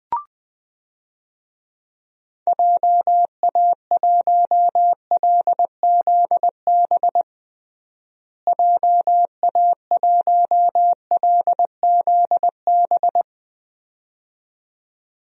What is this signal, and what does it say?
Morse code, heard as a keyed tone, about 20 wpm, 690 Hz. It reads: JA1LZB JA1LZB
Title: DXCC-100 Call Signs - 6 Characters - Repeated 2x - 20wpm